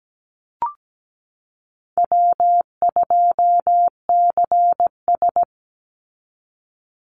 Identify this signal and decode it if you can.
Morse code, heard as a keyed tone, about 17 wpm, 700 Hz. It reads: W2CS
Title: Easy Call Signs - 17wpm